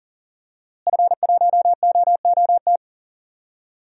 Morse code, heard as a keyed tone, about 40 wpm, 700 Hz. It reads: F1OOT
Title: DXCC-100 Call Signs - 5 Characters - Repeated 2x - 40wpm